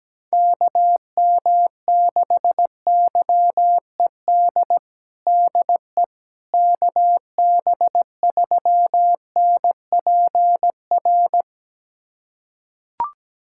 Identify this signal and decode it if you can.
Morse code, heard as a keyed tone, about 17 wpm, 700 Hz. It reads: KM6YED DE KB3NPR
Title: USA Callsign DE Callsign - 6 Characters - 17wpm